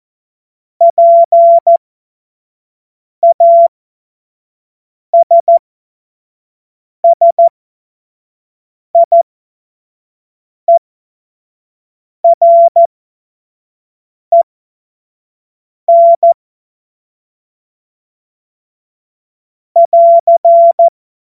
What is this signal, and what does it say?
Morse code, heard as a keyed tone, about 14 wpm, spaced as for 5 wpm, 680 Hz. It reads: PASSIEREN <AR>
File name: Witz-09-September-2025-25-F70